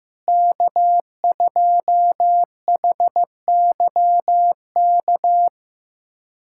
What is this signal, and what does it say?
Morse code, heard as a keyed tone, about 15 wpm, 700 Hz. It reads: K2HYK
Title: USA Callsign DE Callsign 15wpm